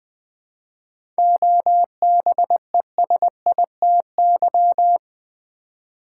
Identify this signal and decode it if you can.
Morse code, heard as a keyed tone, about 20 wpm, 700 Hz. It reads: OBESITY
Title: 7 Letter Words - First 3 Letters Given - 20wpm